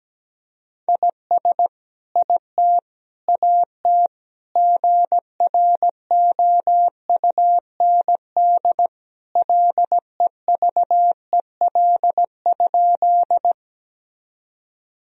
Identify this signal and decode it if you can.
Morse code, heard as a keyed tone, about 17 wpm, 700 Hz. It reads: IS IT AT GROUND LEVEL?
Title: Sentences from Top 1000 Words - Review Part 4 - 17wpm